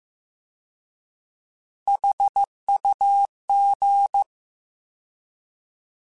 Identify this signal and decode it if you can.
Morse code, heard as a keyed tone, about 15 wpm, 800 Hz. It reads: HUG
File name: Alphabetical / Alphabetical mp3s / Hs @13